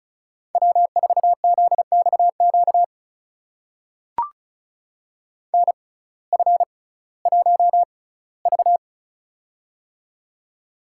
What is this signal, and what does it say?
Morse code, heard as a keyed tone, about 34 wpm, spaced as for 19 wpm, 700 Hz. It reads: W4ZXQ NF1V
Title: USA Call Signs - 6x Character Spacing - 35wpm